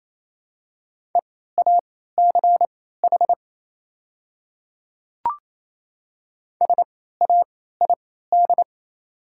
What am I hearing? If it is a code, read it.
Morse code, heard as a keyed tone, about 28 wpm, spaced as for 16 wpm, 700 Hz. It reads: EACH SAID
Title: 3x Character Spacing - Top 100 Words - 28wpm